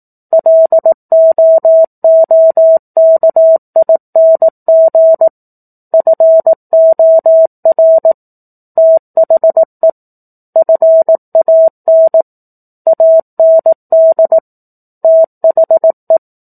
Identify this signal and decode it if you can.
Morse code, heard as a keyed tone, about 18 wpm, 650 Hz. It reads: LOOKING FOR THE FAN AND THE
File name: 04-Chapter-04 18 WPM